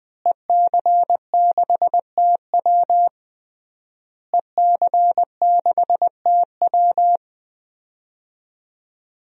Morse code, heard as a keyed tone, about 20 wpm, 700 Hz. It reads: EC6TW EC6TW
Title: DXCC-100 Call Signs - 5 Characters - Repeated 3x - 20wpm